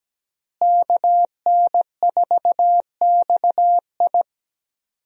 Morse code, heard as a keyed tone, about 17 wpm, 700 Hz. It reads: KN4XI
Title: USA Call Signs - 5 Characters - Repeated 3x - 17wpm